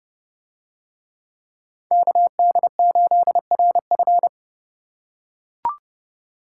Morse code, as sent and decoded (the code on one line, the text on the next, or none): -.- -.. ---.. .-. ..-.
KD8RF